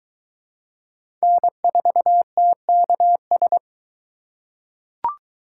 Morse code, heard as a keyed tone, about 23 wpm, 700 Hz. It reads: N4TKS